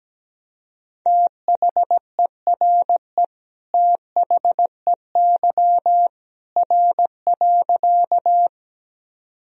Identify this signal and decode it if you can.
Morse code, heard as a keyed tone, about 17 wpm, 700 Hz. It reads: THERE THEY R.
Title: Sentences from Top 100 Words and Abbreviations 17wpm